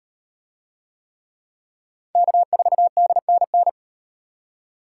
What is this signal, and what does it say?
Morse code, heard as a keyed tone, about 34 wpm, 700 Hz. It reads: K4DNN